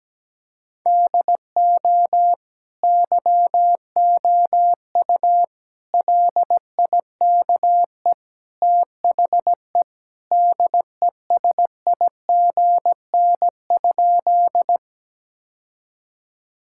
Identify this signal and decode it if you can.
Morse code, heard as a keyed tone, about 17 wpm, 700 Hz. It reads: DO YOU LIKE THE DESIGN?
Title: Sentences from Top 1000 Words - Review Part 5 - 17wpm